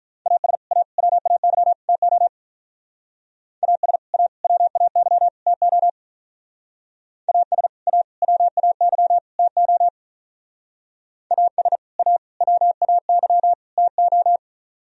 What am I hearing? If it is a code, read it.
Morse code, heard as a keyed tone, about 43 wpm, 700 Hz. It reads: AS A WAY TO AS A WAY TO AS A WAY TO AS A WAY TO
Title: Speed-Racing - Sets of 4 Words 35wpm